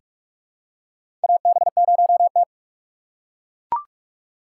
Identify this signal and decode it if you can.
Morse code, heard as a keyed tone, about 42 wpm, 700 Hz. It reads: AB0T
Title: USA Call Signs - 4 Characters - 4x Character Spacing - 45wpm